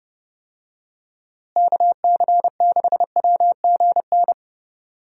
Morse code, heard as a keyed tone, about 30 wpm, 700 Hz. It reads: KC6WGN
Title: USA Call Signs - 6 Characters - Repeated 2x - 30wpm